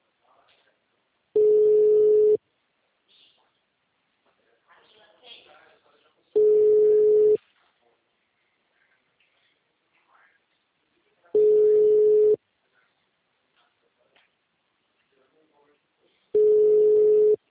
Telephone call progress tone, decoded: ringback tone

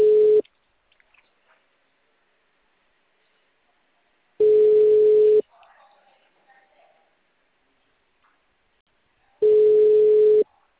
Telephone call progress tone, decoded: ringback tone

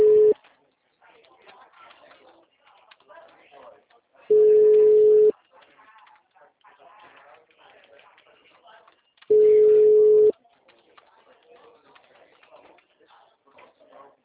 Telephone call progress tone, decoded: ringback tone

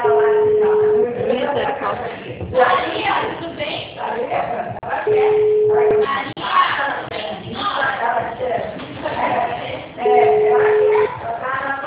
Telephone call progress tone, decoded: ringback tone

